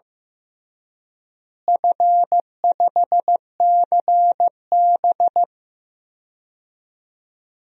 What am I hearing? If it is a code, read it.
Morse code, heard as a keyed tone, about 16 wpm, 700 Hz. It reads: F5CB
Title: DXCC-100 Call Signs - 4 Characters - Repeated 2x - 15wpm